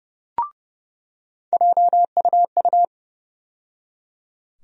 Morse code, heard as a keyed tone, about 32 wpm, 700 Hz. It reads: JUU
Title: USA Call Sign Postfixes - 3 Letters 30wpm